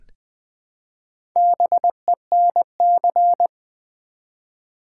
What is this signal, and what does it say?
Morse code, heard as a keyed tone, about 20 wpm, 700 Hz. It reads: BENC
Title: Build A Word - 1 Letter At A Time - 3 to 5-Letter Words - 20wpm